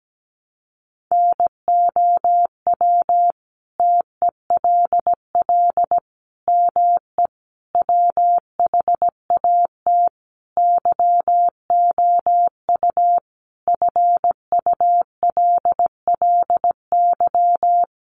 Morse code, heard as a keyed tone, about 17 wpm, 700 Hz. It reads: NOW TELL ME WHAT YOU FULLY
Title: Sentences from Top 1300 Words 17wpm